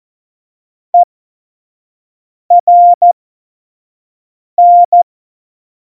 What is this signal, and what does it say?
Morse code, heard as a keyed tone, about 14 wpm, spaced as for 5 wpm, 700 Hz. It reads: ERN
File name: Morsekurs / Witze / Witz 21 April 2020, 25 F70